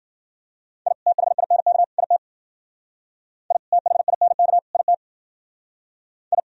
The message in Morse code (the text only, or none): I THINK IT I THINK IT I